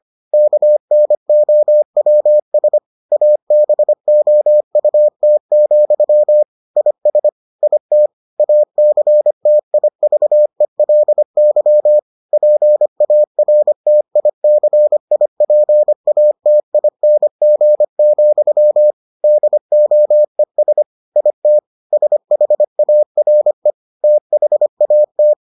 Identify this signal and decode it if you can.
Morse code, heard as a keyed tone, about 25 wpm, 600 Hz. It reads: KNOWS ABOUT, IS IT ACTIVELY PARTICIPATING, DOES IT SHARE THAT